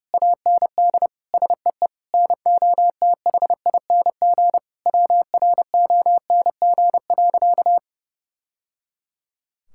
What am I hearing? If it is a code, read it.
Morse code, heard as a keyed tone, about 30 wpm, 700 Hz. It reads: AND SEE NOTHING WRONG.
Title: Sentences from Top 1000 Words - Review Part 6 - 30wpm